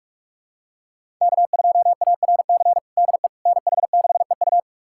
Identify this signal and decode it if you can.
Morse code, heard as a keyed tone, about 46 wpm, 700 Hz. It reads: K2ARC DE NH6EU
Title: USA Callsign DE Callsign - 5 Characters - 45wpm